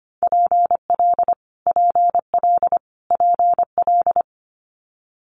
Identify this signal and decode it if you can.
Morse code, heard as a keyed tone, about 25 wpm, 700 Hz. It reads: PL PL PL
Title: Multi-letter Phonemes Repeated 3x 25wpm